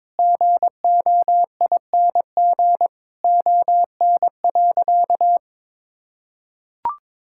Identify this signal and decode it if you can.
Morse code, heard as a keyed tone, about 21 wpm, 700 Hz. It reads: GOING ON.
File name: practice / 22wpm / Sentences with 8-Word Phrases - 22wpm